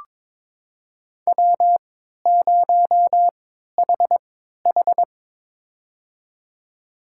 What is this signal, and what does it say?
Morse code, heard as a keyed tone, about 22 wpm, spaced as for 13 wpm, 700 Hz. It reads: W0HH
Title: USA Call Signs - 4 Characters - 3x Character Spacing - 22wpm